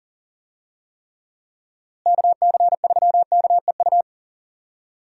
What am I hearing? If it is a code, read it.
Morse code, heard as a keyed tone, about 40 wpm, 700 Hz. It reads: KC3KEU
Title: USA Call Signs - 6 Characters - Repeated 3x - 40wpm